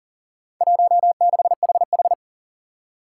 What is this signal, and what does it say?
Morse code, heard as a keyed tone, about 39 wpm, 700 Hz. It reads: 16HH
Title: Four Challenging Letter-Number Combinations 40wpm